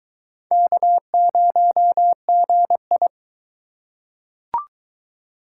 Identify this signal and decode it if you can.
Morse code, heard as a keyed tone, about 21 wpm, 700 Hz. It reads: K0GI